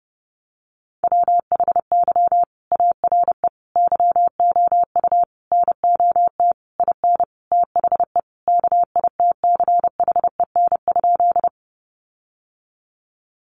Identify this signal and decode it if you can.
Morse code, heard as a keyed tone, about 30 wpm, 700 Hz. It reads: WHY ARE YOU NOT IN THE KITCHEN?
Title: Sentences from Top 1400 Words 30wpm